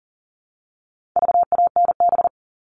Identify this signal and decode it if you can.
Morse code, heard as a keyed tone, about 40 wpm, 700 Hz. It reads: VANB
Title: All Cut Numbers - 4-Digit 40wpm